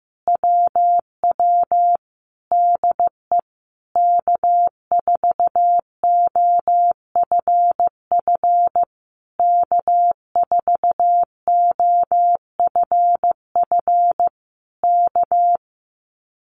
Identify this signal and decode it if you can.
Morse code, heard as a keyed tone, about 15 wpm, 700 Hz. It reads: WW DE K4OFF K4OFF K
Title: Common QSO Phrases Encore 15wpm